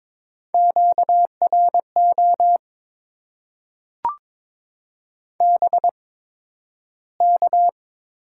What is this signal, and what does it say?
Morse code, heard as a keyed tone, about 22 wpm, spaced as for 12 wpm, 690 Hz. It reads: QRO BK
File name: Top 90 QSO Elements - 8x Character Spacing - 22wpm